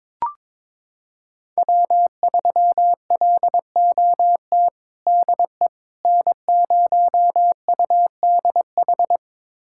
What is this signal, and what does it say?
Morse code, heard as a keyed tone, about 22 wpm, 700 Hz. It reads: W3LOT DE N0UDH
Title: USA Callsign DE Callsign - 5 Characters - 22wpm